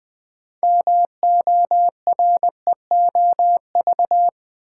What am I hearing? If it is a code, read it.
Morse code, heard as a keyed tone, about 20 wpm, 700 Hz. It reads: MOREOV
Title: Build A Word - 3 Letters At A Time - 8-Letter Words - 20wpm